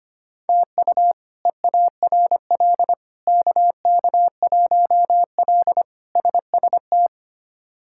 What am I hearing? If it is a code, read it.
Morse code, heard as a keyed tone, about 25 wpm, 700 Hz. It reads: TU EARL KK1L SST
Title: K1USN SST QSO Practice - 25wpm